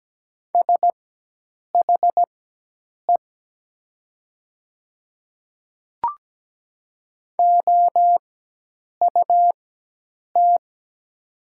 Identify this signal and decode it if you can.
Morse code, heard as a keyed tone, about 17 wpm, spaced as for 8 wpm, 700 Hz. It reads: SHE OUT
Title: 4x Character Spacing - Top 500 Words - 17wpm